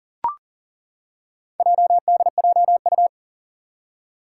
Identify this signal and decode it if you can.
Morse code, heard as a keyed tone, about 40 wpm, 700 Hz. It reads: JDJU
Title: Four Challenging Letter-Number Combinations 40wpm